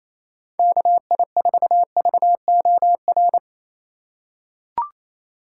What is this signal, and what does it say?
Morse code, heard as a keyed tone, about 27 wpm, 700 Hz. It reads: KI4VOR
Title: USA Call Signs - 6 Characters - 4x Character Spacing - 28wpm